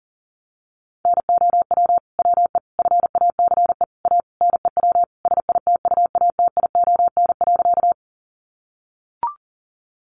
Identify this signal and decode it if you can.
Morse code, heard as a keyed tone, about 40 wpm, 700 Hz. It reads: NOW WE FACE A NEW SITUATION.
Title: Sentences from Top 1200 Words 40wpm